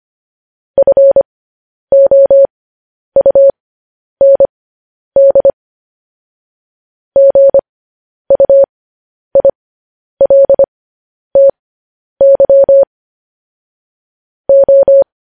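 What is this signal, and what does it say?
Morse code, heard as a keyed tone, about 25 wpm, spaced as for 10 wpm, 550 Hz. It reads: FOUND GUILTY O
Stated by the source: Morse code practice sending news headlines at 10 wpm.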